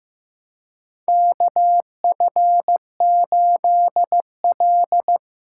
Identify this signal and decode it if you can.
Morse code, heard as a keyed tone, about 15 wpm, 700 Hz. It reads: KF8L
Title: USA Callsign DE Callsign - 4 Characters - 15wpm